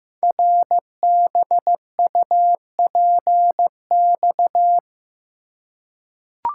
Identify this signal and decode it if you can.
Morse code, heard as a keyed tone, about 15 wpm, 700 Hz. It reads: RBUPX
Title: Code Groups - 5 Letters - 15wpm